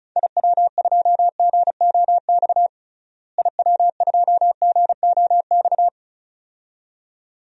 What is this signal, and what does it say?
Morse code, heard as a keyed tone, about 34 wpm, 700 Hz. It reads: IW2GOX IW2GOX
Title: DXCC-100 Call Signs - 6 Characters - Repeated 2x - 35wpm